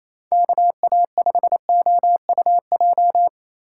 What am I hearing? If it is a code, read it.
Morse code, heard as a keyed tone, about 28 wpm, 700 Hz. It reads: KA5OUJ
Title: USA Call Signs - 6 Characters - 4x Character Spacing - 28wpm